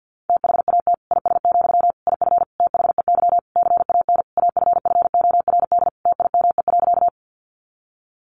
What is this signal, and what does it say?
Morse code, heard as a keyed tone, about 50 wpm, 700 Hz. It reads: THAT IS, IF THEY CAN AFFORD TIME.
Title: Sentences from Top 1100 Words 50wpm